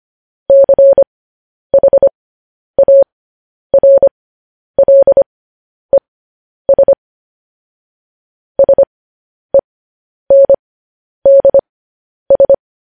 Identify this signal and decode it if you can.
Morse code, heard as a keyed tone, about 25 wpm, spaced as for 10 wpm, 550 Hz. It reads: CHARLES SENDS
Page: News Headlines in Morse Code at 10 WPM